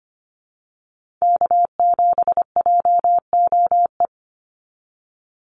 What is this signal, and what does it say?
Morse code, heard as a keyed tone, about 25 wpm, 700 Hz. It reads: K7JOE
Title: Call Signs Repeated 2x - 25wpm